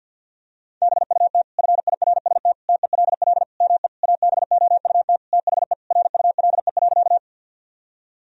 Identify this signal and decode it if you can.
Morse code, heard as a keyed tone, about 50 wpm, 700 Hz. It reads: BUT FIRST TELL ME ABOUT THE RULE.
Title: Sentences from Top 1000 Words - Review Part 2 - 50wpm